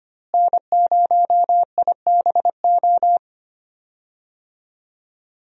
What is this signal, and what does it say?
Morse code, heard as a keyed tone, about 25 wpm, 700 Hz. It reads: N0IBO
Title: USA Call Signs - 5 Characters - Repeated 3x - 25wpm